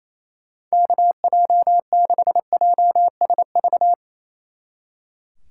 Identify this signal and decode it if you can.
Morse code, heard as a keyed tone, about 34 wpm, 700 Hz. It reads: KJ6JSV
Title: Speed Racing - USA 2x3 Call Signs 28wpm